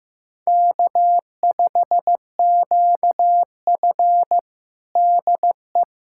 Morse code, heard as a keyed tone, about 15 wpm, 700 Hz. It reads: K5QF DE
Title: USA Callsign DE Callsign - 4 Characters - 15wpm